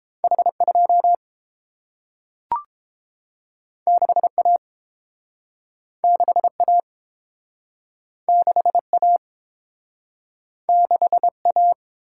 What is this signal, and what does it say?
Morse code, heard as a keyed tone, about 30 wpm, 700 Hz. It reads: H 2 6 A 6 A 6 A 6 A